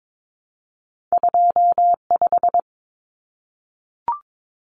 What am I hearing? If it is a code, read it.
Morse code, heard as a keyed tone, about 22 wpm, 700 Hz. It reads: 25
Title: Numbers - 2-Digit 22wpm